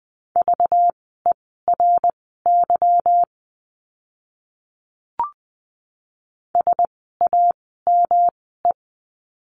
Morse code, heard as a keyed tone, about 20 wpm, spaced as for 15 wpm, 700 Hz. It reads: VERY SAME
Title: 237 Lesson 38f - Letter q - Words 20wpm